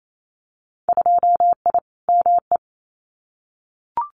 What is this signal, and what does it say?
Morse code, heard as a keyed tone, about 28 wpm, 700 Hz. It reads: 2I ME
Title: Winter Field Day Exchanges Encore 28wpm